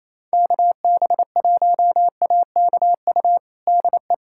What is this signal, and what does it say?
Morse code, heard as a keyed tone, about 28 wpm, 700 Hz. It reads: KB1AKU DE
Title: USA Callsign DE Callsign - 6 Characters - 28wpm